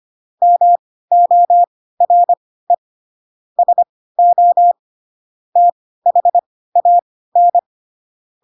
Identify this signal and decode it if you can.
Morse code, heard as a keyed tone, about 25 wpm, spaced as for 16 wpm, 700 Hz. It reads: MORE SO THAN
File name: Pinocchio / Pinocchio ch3 10 25 ik1hge